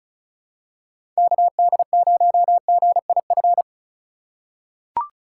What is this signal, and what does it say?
Morse code, heard as a keyed tone, about 36 wpm, 700 Hz. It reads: KD0GIF
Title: USA Call Signs - 6 Characters - 2x Character Spacing - 35wpm